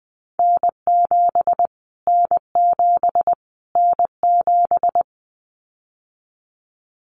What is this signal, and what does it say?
Morse code, heard as a keyed tone, about 20 wpm, 700 Hz. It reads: N7 N7 N7
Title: Callsign Prefixes for Top 25 DXCC - Repeated 3x - 20wpm